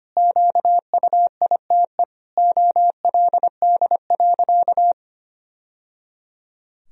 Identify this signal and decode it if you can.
Morse code, heard as a keyed tone, about 25 wpm, 700 Hz. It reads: QUITE OLD.
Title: Sentences from Top 1000 Words - Review Part 3 - 25wpm